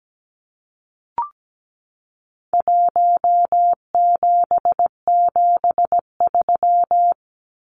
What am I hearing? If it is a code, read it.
Morse code, heard as a keyed tone, about 17 wpm, 700 Hz. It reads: 1773
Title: Numbers - Contest Style - 4-Digit 17wpm